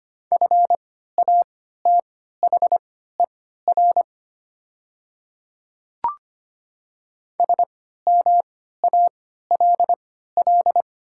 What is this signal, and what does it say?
Morse code, heard as a keyed tone, about 25 wpm, spaced as for 14 wpm, 700 Hz. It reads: FATHER SMALL
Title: 3x Character Spacing - Top 500 Words - 25wpm